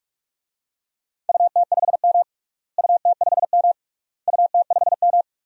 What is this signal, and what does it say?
Morse code, heard as a keyed tone, about 46 wpm, 700 Hz. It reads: UT5M UT5M UT5M
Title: DXCC-100 Call Signs - 4 Characters - Repeated 3x - 45wpm